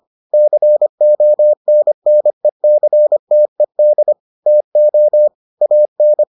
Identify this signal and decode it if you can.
Morse code, heard as a keyed tone, about 25 wpm, 600 Hz. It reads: CONNECTED TO AN